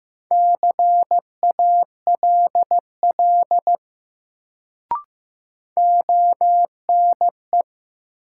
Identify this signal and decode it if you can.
Morse code, heard as a keyed tone, about 15 wpm, 700 Hz. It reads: CALL ONE
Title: Sentences from Top 100 Words - New Word Drill - 15wpm